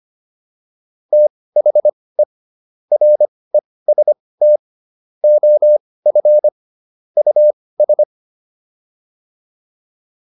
Morse code, heard as a keyed tone, about 25 wpm, spaced as for 18 wpm, 600 Hz. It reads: THE REST OF US